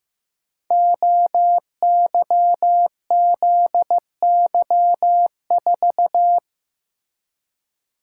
Morse code, heard as a keyed tone, about 15 wpm, 700 Hz. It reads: OYZY4
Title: Code Groups - 5 Letter:Numbers - 15wpm